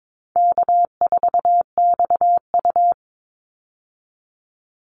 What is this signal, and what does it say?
Morse code, heard as a keyed tone, about 22 wpm, 700 Hz. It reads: K4XU
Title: Easy Call Signs - 22wpm